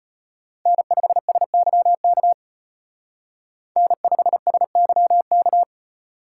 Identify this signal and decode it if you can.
Morse code, heard as a keyed tone, about 33 wpm, 700 Hz. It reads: N5SYK N5SYK